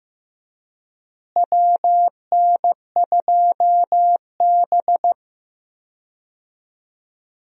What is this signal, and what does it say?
Morse code, heard as a keyed tone, about 15 wpm, 700 Hz. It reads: WN2B
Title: USA Call Signs - 4 Characters - Repeated 3x - 15wpm